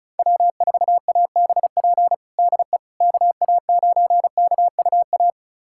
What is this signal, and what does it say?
Morse code, heard as a keyed tone, about 36 wpm, 700 Hz. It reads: W4ABP DE KA9KUA
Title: USA Callsign DE Callsign 35wpm